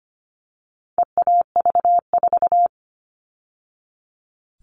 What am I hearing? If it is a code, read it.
Morse code, heard as a keyed tone, about 25 wpm, 700 Hz. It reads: EAV4
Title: All Cut Numbers - 4-Digit 25wpm